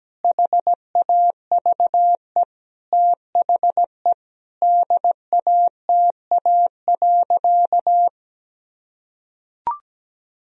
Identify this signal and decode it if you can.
Morse code, heard as a keyed tone, about 17 wpm, 700 Hz. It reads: HAVE THE DATA.
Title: Sentences from Top 1000 Words - Review Part 3 - 17wpm